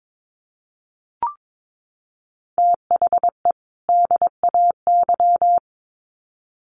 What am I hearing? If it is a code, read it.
Morse code, heard as a keyed tone, about 22 wpm, 700 Hz. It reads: THE DAY
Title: Sets of 2 Words 22wpm